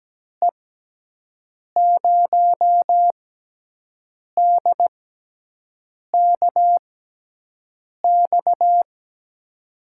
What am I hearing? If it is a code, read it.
Morse code, heard as a keyed tone, about 17 wpm, spaced as for 6 wpm, 700 Hz. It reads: E0DKX
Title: USA Call Signs - 6 Characters - 6x Character Spacing - 17wpm